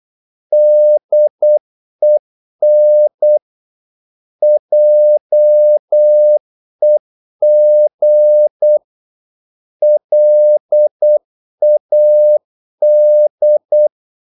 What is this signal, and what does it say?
Morse code, heard as a keyed tone, about 8 wpm, 600 Hz. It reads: DEN JEG LAD
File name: Korte ord DK 8wpm